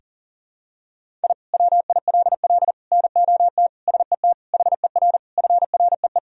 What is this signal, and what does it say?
Morse code, heard as a keyed tone, about 40 wpm, 700 Hz. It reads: I WILL NOT SET HER FREE
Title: Sentences from Top 1000 Words - Review Part 1 - 40wpm